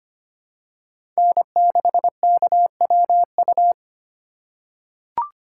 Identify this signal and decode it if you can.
Morse code, heard as a keyed tone, about 25 wpm, 700 Hz. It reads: N6KWU
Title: USA Call Signs - 5 Characters - 4x Character Spacing - 25wpm